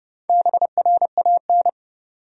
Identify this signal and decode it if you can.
Morse code, heard as a keyed tone, about 32 wpm, 700 Hz. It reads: BRAN
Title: USA Common Names - Max 5 Letters - 30wpm